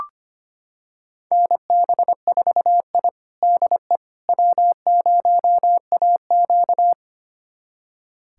Morse code, heard as a keyed tone, about 25 wpm, 700 Hz. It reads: NB4I DE W0AQ
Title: USA Callsign DE Callsign - 4 Characters - 25wpm